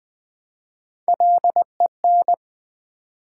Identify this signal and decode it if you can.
Morse code, heard as a keyed tone, about 20 wpm, 700 Hz. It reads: LEN
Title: Build A Word - 1 Letter At A Time - 4-Letter Words - 20wpm